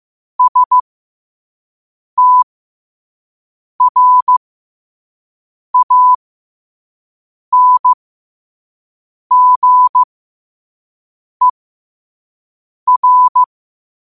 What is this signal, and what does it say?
Morse code, heard as a keyed tone, about 15 wpm, spaced as for 5 wpm, 1000 Hz. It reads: STRANGER
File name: qotdfiles / 5wpm